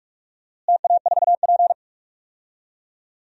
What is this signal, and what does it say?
Morse code, heard as a keyed tone, about 44 wpm, 700 Hz. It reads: TA4P
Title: DXCC-100 Call Signs - 4 Characters - Repeated 2x - 45wpm